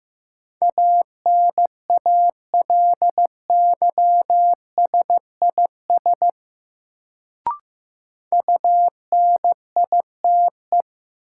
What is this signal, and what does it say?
Morse code, heard as a keyed tone, about 15 wpm, 700 Hz. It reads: ANALYSIS UNITE
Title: Sentences from Top 1000 Words - Review Part 5 - New Word Drill - 15wpm